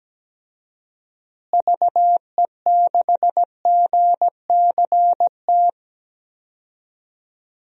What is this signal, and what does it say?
Morse code, heard as a keyed tone, about 17 wpm, 700 Hz. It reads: VE6GCT
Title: DXCC-100 Call Signs - 6 Characters - Repeated 2x - 17wpm